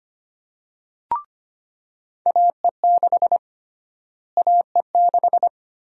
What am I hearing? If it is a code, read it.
Morse code, heard as a keyed tone, about 25 wpm, 700 Hz. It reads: AE6 AE6
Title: USA Call Sign Prefixes - Repeated 2x - 25wpm